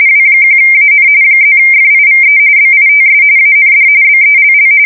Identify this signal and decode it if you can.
RTTY, about 45.45 baud. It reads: W1AN +FIVE BITS AND BELL